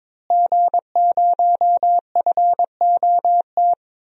Morse code, heard as a keyed tone, about 22 wpm, 700 Hz. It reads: G0FOT
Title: DXCC-100 Call Signs - 5 Characters - Repeated 3x - 22wpm